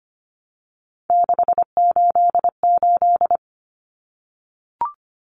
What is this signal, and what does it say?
Morse code, heard as a keyed tone, about 25 wpm, 700 Hz. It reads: 688
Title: Numbers - 3-Digit 25wpm